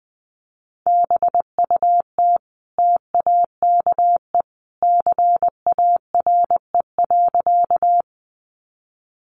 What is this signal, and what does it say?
Morse code, heard as a keyed tone, about 20 wpm, 700 Hz. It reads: BUT TAKE CARE.
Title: Sentences from Top 300 Words 20wpm